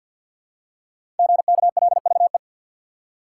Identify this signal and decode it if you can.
Morse code, heard as a keyed tone, about 52 wpm, 700 Hz. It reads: GKLVE
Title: Code Groups - 5 Letter:Numbers - 50wpm